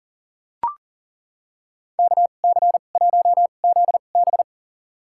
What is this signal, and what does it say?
Morse code, heard as a keyed tone, about 40 wpm, spaced as for 29 wpm, 700 Hz. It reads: KC1ZB
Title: USA Call Signs - 5 Characters - 2x Character Spacing - 40wpm